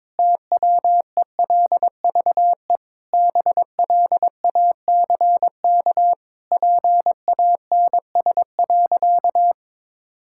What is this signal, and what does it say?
Morse code, heard as a keyed tone, about 22 wpm, 700 Hz. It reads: TWELVE BLACK PANS.
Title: Pangram Challenge v2 - 22wpm